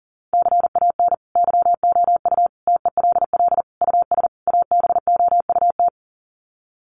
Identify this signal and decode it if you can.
Morse code, heard as a keyed tone, about 40 wpm, 700 Hz. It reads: CAN YOU TELL US ABOUT
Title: Sets of 5 Words Encore2 40wpm